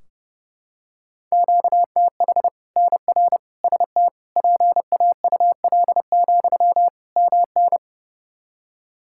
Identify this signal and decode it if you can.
Morse code, heard as a keyed tone, about 32 wpm, 700 Hz. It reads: QTH NR ST PAUL, MN
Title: K1USN SST QSO Practice 30wpm